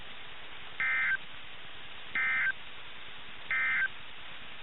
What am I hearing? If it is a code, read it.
Emergency Alert System (SAME) tones, heard as harsh buzzing bursts. It is heard from 0.7 to 4.0 s.
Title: NOAA Weather Radio - KIH61